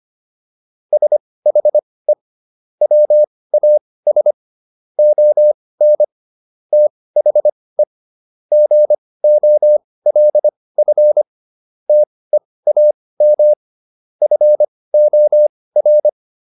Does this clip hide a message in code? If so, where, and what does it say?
Yes: Morse code from 0.9 to 16.1 s, heard as a keyed tone, about 25 wpm, spaced as for 18 wpm, 600 Hz. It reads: SHE WAS ON THE GOLF TEAM FOR